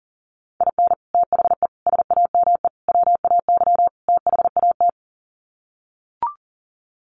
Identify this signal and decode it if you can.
Morse code, heard as a keyed tone, about 40 wpm, 700 Hz. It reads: IN THE SAME WAY THAT
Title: Sets of 5 Words 40wpm